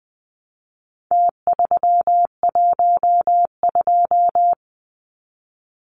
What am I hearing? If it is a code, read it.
Morse code, heard as a keyed tone, about 20 wpm, 700 Hz. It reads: T312
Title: Numbers - Contest Style - 4-Digit 20wpm